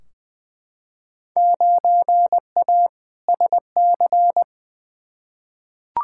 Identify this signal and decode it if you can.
Morse code, heard as a keyed tone, about 21 wpm, 700 Hz. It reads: 9A SC
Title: Field Day Exchanges Encore - 20wpm